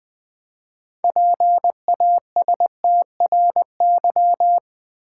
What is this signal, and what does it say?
Morse code, heard as a keyed tone, about 20 wpm, 700 Hz. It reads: PASTRY
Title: Easy Words with Hints - 20wpm